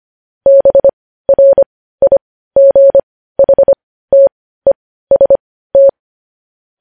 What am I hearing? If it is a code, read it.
Morse code, heard as a keyed tone, about 25 wpm, spaced as for 15 wpm, 550 Hz. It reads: BRIGHTEST
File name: podcast / News headlines-1x-25x15